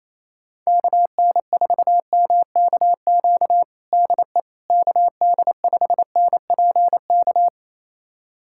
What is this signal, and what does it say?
Morse code, heard as a keyed tone, about 27 wpm, 700 Hz. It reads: KN4MKQ DE KD5NPK
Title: USA Callsign DE Callsign - 6 Characters - 28wpm